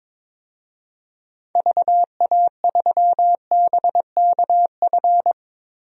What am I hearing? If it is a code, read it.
Morse code, heard as a keyed tone, about 22 wpm, 700 Hz. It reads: VA3BKF
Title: DXCC-100 Call Signs - 6 Characters - Repeated 3x - 22wpm